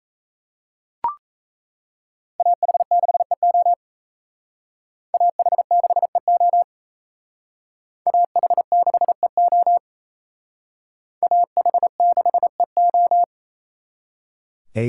Morse code, heard as a keyed tone, about 35 wpm, 700 Hz. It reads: AH6EO AH6EO AH6EO AH6EO